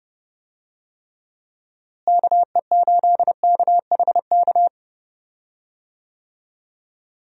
Morse code, heard as a keyed tone, about 30 wpm, 700 Hz. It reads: KE8KHK
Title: USA Call Signs - 6 Characters - Repeated 3x - 30wpm